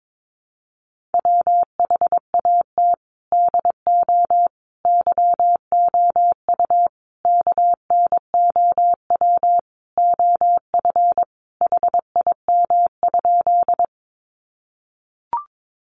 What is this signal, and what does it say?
Morse code, heard as a keyed tone, about 22 wpm, 700 Hz. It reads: WHAT DO YOU KNOW OF HIM?
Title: Sentences from Top 100 Words 22wpm